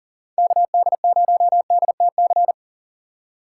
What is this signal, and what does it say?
Morse code, heard as a keyed tone, about 40 wpm, 700 Hz. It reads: KD0DTC
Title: USA Call Signs - 6 Characters - 6x Character Spacing - 40wpm